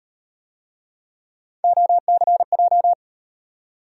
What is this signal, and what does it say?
Morse code, heard as a keyed tone, about 38 wpm, 700 Hz. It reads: OCJ